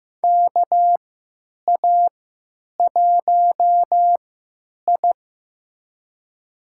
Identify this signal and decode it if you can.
Morse code, heard as a keyed tone, about 15 wpm, spaced as for 9 wpm, 700 Hz. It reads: KA1I
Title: USA Call Signs - 3x Character Spacing - 15wpm